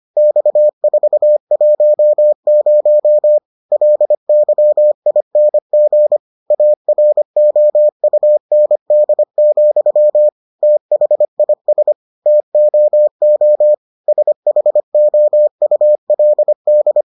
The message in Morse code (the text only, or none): X410 LYING AROUND, THIS TOO SHOULD